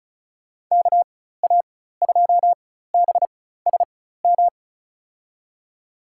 Morse code, heard as a keyed tone, about 34 wpm, spaced as for 16 wpm, 700 Hz. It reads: KA2BSM
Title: USA Call Signs - 6 Characters - 4x Character Spacing - 35wpm